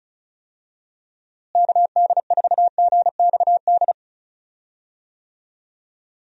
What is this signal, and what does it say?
Morse code, heard as a keyed tone, about 34 wpm, 700 Hz. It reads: KD4GXD
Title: USA Call Signs - 6 Characters - Repeated 2x - 35wpm